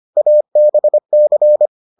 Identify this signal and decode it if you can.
Morse code, heard as a keyed tone, about 25 wpm, 600 Hz. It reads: ABC